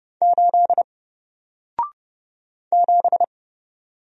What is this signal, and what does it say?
Morse code, heard as a keyed tone, about 30 wpm, 700 Hz. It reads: 8 7
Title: Single Character (Letter:Number:Punctuation:Prosign) Alt 30wpm